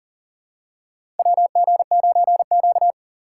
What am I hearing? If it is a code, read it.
Morse code, heard as a keyed tone, about 40 wpm, 700 Hz. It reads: WG9Q
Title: USA Call Signs - 4 Characters - 6x Character Spacing - 40wpm